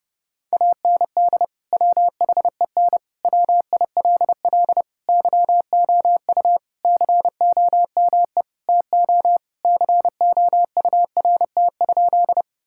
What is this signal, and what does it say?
Morse code, heard as a keyed tone, about 32 wpm, 700 Hz. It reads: AND WHEN WILL YOU COME TO COURT?
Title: Sentences from Top 1600 Words 30wpm